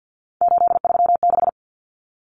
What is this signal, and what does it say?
Morse code, heard as a keyed tone, about 50 wpm, 700 Hz. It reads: 836
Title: Common Cut Numbers - 3-Digit 50wpm